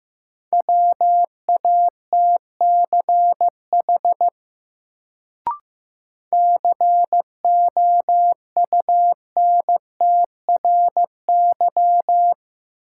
Morse code, heard as a keyed tone, about 15 wpm, 700 Hz. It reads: WATCH COUNTRY
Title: Sentences from Top 300 Words - New Word Drill - 15wpm